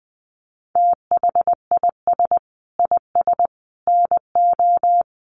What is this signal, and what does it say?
Morse code, heard as a keyed tone, about 20 wpm, 700 Hz. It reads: THIS IS NO
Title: Sentences from Top 700 Words 20wpm